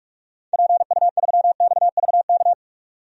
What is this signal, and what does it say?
Morse code, heard as a keyed tone, about 43 wpm, 700 Hz. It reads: PU3XVK
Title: DXCC-100 Call Signs - 6 Characters - 45wpm